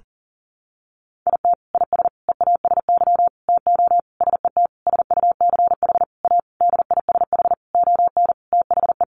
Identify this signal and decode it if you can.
Morse code, heard as a keyed tone, about 40 wpm, 700 Hz. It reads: IT IS EASY TO SET SUCH A DISH ON THE
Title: Sentences from Top 1300 Words 40wpm